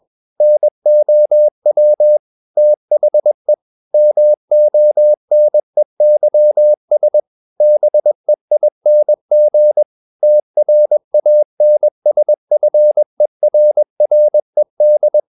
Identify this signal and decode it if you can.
Morse code, heard as a keyed tone, about 21 wpm, 600 Hz. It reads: NOW THE MONEYS BEING TRANSFERRED
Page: CW med Gnister